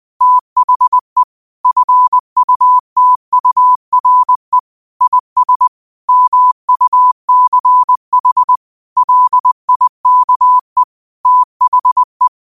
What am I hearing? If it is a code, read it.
Morse code, heard as a keyed tone, about 20 wpm, 1000 Hz. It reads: THE FUTURE IS MUCH LIKE THE
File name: qotdfiles / 20wpm